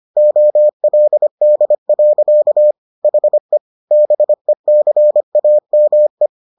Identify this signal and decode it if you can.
Morse code, heard as a keyed tone, about 25 wpm, 600 Hz. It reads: OLD. HE BECAME